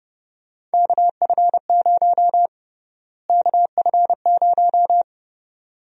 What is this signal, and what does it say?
Morse code, heard as a keyed tone, about 30 wpm, 700 Hz. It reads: KF0 KF0
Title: USA Call Sign Prefixes - Repeated 2x - 30wpm